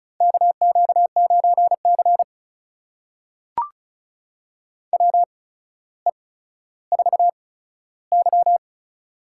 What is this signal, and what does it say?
Morse code, heard as a keyed tone, about 36 wpm, spaced as for 15 wpm, 700 Hz. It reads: KQ9C WE4Y
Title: USA Call Signs - 4 Characters - 8x Character Spacing - 35wpm